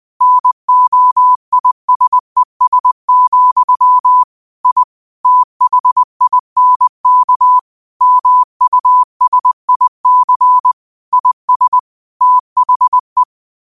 Morse code, heard as a keyed tone, about 20 wpm, 1000 Hz. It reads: NOISES, I THINK MUSIC IS THE